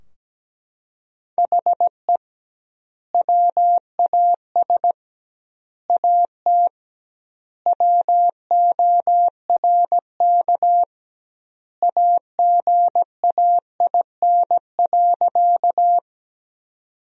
Morse code, heard as a keyed tone, about 17 wpm, 700 Hz. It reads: HE WAS AT WORK AGAIN.
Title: 2x Word-Spacing - Sentences from Top 200 Words - 17wpm